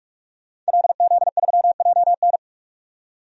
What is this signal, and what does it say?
Morse code, heard as a keyed tone, about 48 wpm, 700 Hz. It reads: LZ3JN
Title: DXCC-100 Call Signs - 5 Characters - 45wpm